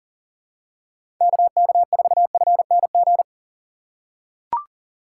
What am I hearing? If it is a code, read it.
Morse code, heard as a keyed tone, about 38 wpm, 700 Hz. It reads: KK4FNG
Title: USA Call Signs - 4x Character Spacing - 40wpm